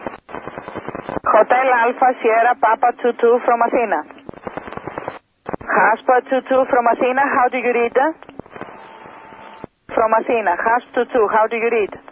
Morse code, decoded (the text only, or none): none